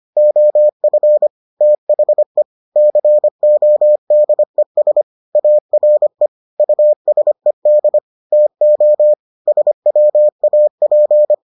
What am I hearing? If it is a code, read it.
Morse code, heard as a keyed tone, about 25 wpm, 600 Hz. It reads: OF THE CODES ARE USED TO SWAP